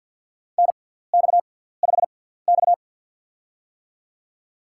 Morse code, heard as a keyed tone, about 53 wpm, spaced as for 17 wpm, 700 Hz. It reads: NX5X
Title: USA Call Signs - 6x Character Spacing - 50wpm